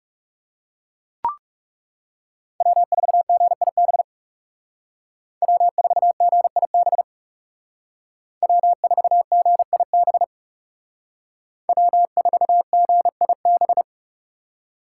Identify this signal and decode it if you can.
Morse code, heard as a keyed tone, about 37 wpm, 700 Hz. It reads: W4GIB W4GIB W4GIB W4GIB
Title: Speed Racing - USA 1x3 Call Signs 30wpm